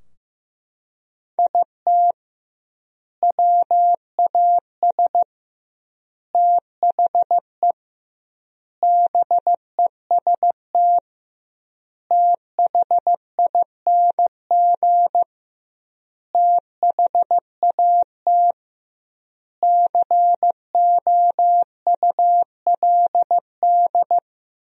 Morse code, heard as a keyed tone, about 15 wpm, 700 Hz. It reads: IT WAS THE BEST THING THAT COULD
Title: 2x Word-Spacing - Sentences from Top 400 Words - 15wpm